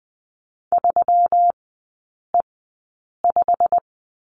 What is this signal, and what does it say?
Morse code, heard as a keyed tone, about 20 wpm, spaced as for 8 wpm, 700 Hz. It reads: 3E5
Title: 212 Lesson 34e - Number 8 - Three Characters 20wpm